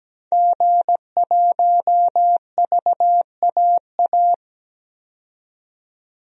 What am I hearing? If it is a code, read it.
Morse code, heard as a keyed tone, about 17 wpm, 700 Hz. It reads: G1VAA
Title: DXCC-100 Call Signs - 5 Characters - 17wpm